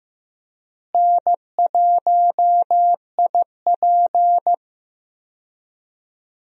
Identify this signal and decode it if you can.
Morse code, heard as a keyed tone, about 15 wpm, 700 Hz. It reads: N1IP
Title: USA Call Signs - 4 Characters - Repeated 3x - 15wpm